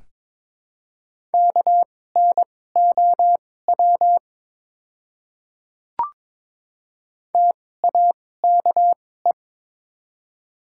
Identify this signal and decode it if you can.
Morse code, heard as a keyed tone, about 22 wpm, spaced as for 16 wpm, 700 Hz. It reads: KNOW TAKE
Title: 2x Character Spacing - Top 500 Words - 22wpm